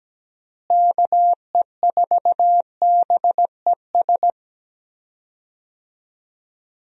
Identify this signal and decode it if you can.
Morse code, heard as a keyed tone, about 17 wpm, 700 Hz. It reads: KE4BES
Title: USA Call Signs - 6 Characters - Repeated 2x - 17wpm